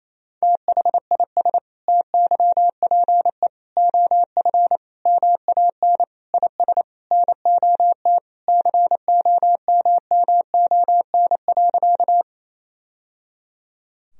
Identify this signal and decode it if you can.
Morse code, heard as a keyed tone, about 28 wpm, 700 Hz. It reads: THIS TYPE OF MAN IS NOT COMMON.
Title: Sentences from Top 1000 Words - Review Part 3 - 28wpm